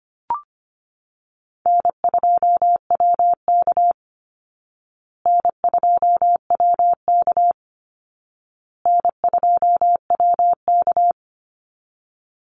Morse code, heard as a keyed tone, about 25 wpm, 700 Hz. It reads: N2WK N2WK N2WK
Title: Easy Call Signs Repeated 3x - 25wpm